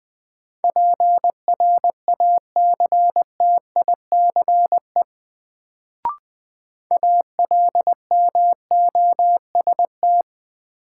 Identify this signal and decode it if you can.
Morse code, heard as a keyed tone, about 21 wpm, 700 Hz. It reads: PRACTICE ALMOST
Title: Sentences from Top 700 Words - New Word Drill - 20wpm